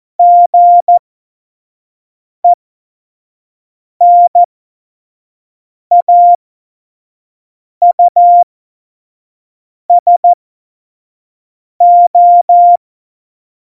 Morse code, heard as a keyed tone, about 14 wpm, spaced as for 5 wpm, 700 Hz. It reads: GENAUSO
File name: Morsekurs / Witze / Witz 16. April 2019, 25 F70